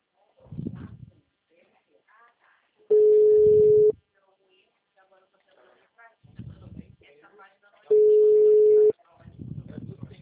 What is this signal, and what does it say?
Ringback tone (Europe, 425 Hz), with speech.